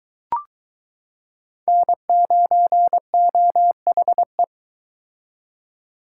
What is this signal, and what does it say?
Morse code, heard as a keyed tone, about 22 wpm, 700 Hz. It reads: N9OHE